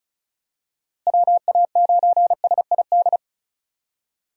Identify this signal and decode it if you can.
Morse code, heard as a keyed tone, about 36 wpm, 700 Hz. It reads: WA9SID
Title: USA Call Signs - 6 Characters - Repeated 2x - 35wpm